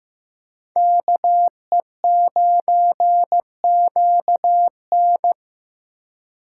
Morse code, heard as a keyed tone, about 18 wpm, 700 Hz. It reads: KE9QN
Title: Speed Racing - USA Call Signs 15wpm